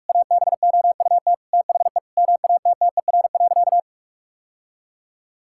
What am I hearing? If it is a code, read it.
Morse code, heard as a keyed tone, about 41 wpm, 700 Hz. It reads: ABOUT THE MATTER.
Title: Sentences from Top 1000 Words - Review Part 5 - 45wpm